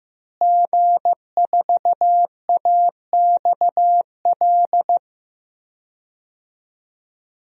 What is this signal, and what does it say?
Morse code, heard as a keyed tone, about 15 wpm, 700 Hz. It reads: G4AXL
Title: DXCC-100 Call Signs - 5 Characters - Repeated 3x - 15wpm